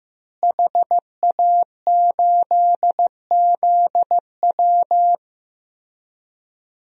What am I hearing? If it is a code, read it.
Morse code, heard as a keyed tone, about 15 wpm, 700 Hz. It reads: HA8ZW
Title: DXCC-100 Call Signs - 5 Characters - Repeated 2x - 15wpm